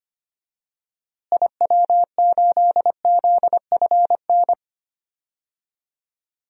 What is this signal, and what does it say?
Morse code, heard as a keyed tone, about 25 wpm, 700 Hz. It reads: IW8ZFN
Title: DXCC-100 Call Signs - 6 Characters - Repeated 3x - 25wpm